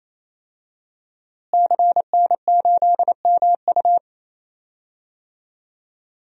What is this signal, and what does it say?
Morse code, heard as a keyed tone, about 28 wpm, 700 Hz. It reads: CN8MU
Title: DXCC-100 Call Signs - 5 Characters - Repeated 3x - 28wpm